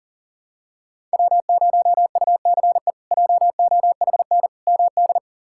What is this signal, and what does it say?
Morse code, heard as a keyed tone, about 40 wpm, 700 Hz. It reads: W0UCE JOHN MD
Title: K1USN SST QSO Practice - 40wpm